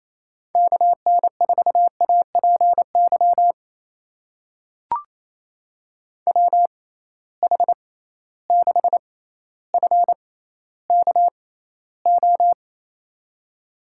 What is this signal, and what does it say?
Morse code, heard as a keyed tone, about 28 wpm, spaced as for 14 wpm, 700 Hz. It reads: KN4APY WH6FKO
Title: USA Call Signs - 6 Characters - 6x Character Spacing - 28wpm